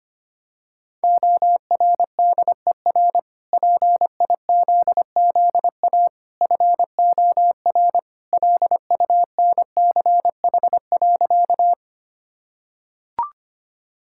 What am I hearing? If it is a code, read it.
Morse code, heard as a keyed tone, about 25 wpm, 700 Hz. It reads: ORDER PIZZA FOR LUNCH.
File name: Easy Sentences with Hints - 25wpm